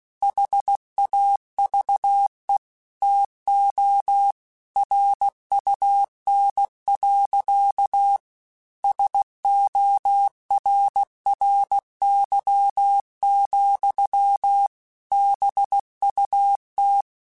Morse code, heard as a keyed tone, about 16 wpm, 800 Hz. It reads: HAVE TO RUN. SORRY, BUT
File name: QSOs / QSOs-mp3s / WA7QOX 2@15